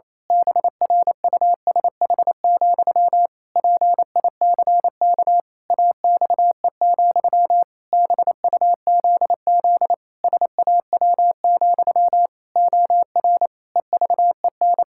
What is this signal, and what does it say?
Morse code, heard as a keyed tone, about 28 wpm, 700 Hz. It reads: BRUSH, PICK AXE, BUZZ SAW, OR EVEN
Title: Pangram Challenge v2 - 28wpm